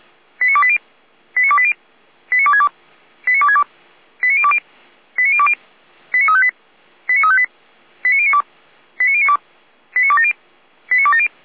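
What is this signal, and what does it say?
Five-tone selective call (ZVEI), heard as quick runs of five tones, tones of about 70 ms. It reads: 89280 89280 89262 89262 89020 89020 89358 89358 89092 89092 89270 89270
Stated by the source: ZVEI five-tone call received by SDR, from the feuerwehr-orlamuende.org audio server.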